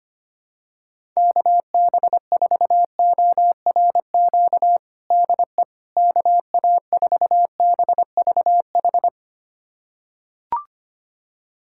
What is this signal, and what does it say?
Morse code, heard as a keyed tone, about 25 wpm, 700 Hz. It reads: KB4ORQ DE KA4BVH
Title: USA Callsign DE Callsign - 6 Characters - 25wpm